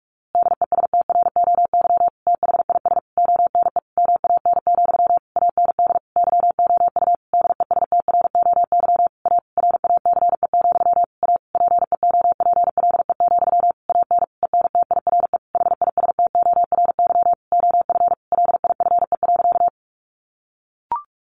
Morse code, heard as a keyed tone, about 44 wpm, 700 Hz. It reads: DESTROY THIS ONE MAN, AND YOU DESTROY A RACE, A PEOPLE, AN ENTIRE HISTORY OF LIFE.
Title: A Sound Of Thunder 45wpm